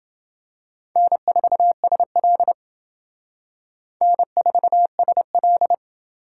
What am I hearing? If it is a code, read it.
Morse code, heard as a keyed tone, about 29 wpm, 700 Hz. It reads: N4SL N4SL